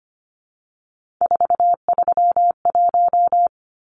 Morse code, heard as a keyed tone, about 25 wpm, 700 Hz. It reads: 431
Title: Numbers - 3-Digit 25wpm